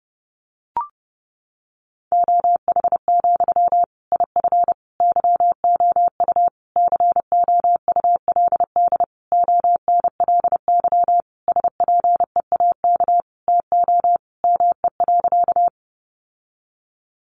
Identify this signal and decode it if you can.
Morse code, heard as a keyed tone, about 30 wpm, 700 Hz. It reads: OH, IF YOU COULD ONLY SPEAK TO ME.
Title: Sentences from Top 1100 Words 30wpm